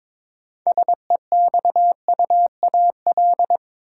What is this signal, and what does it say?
Morse code, heard as a keyed tone, about 22 wpm, 700 Hz. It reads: SEXUAL
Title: Six Letter Words - 22wpm